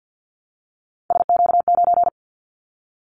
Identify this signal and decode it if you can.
Morse code, heard as a keyed tone, about 50 wpm, 700 Hz. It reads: SQ9
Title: Callsign Prefixes for Top 25 DXCC - 50wpm